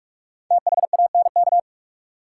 Morse code, heard as a keyed tone, about 46 wpm, 700 Hz. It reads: THANK
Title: Sets of 2 Words - New Word Drill - 45wpm